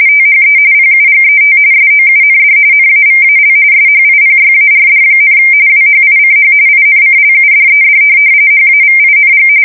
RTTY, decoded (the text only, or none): THE PUGET SOUND NAVAL SHIPYARD, WHICH INCLUDED HAVING